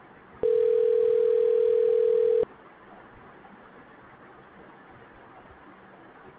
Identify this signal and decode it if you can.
Ringback tone (North America), 440 + 480 Hz.